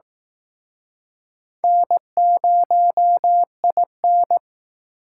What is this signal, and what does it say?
Morse code, heard as a keyed tone, about 18 wpm, 700 Hz. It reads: N0IN
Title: Speed Racing - USA 1x2 Call Signs 15wpm